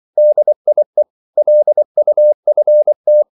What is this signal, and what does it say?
Morse code, heard as a keyed tone, about 24 wpm, 600 Hz. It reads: DIE LUFT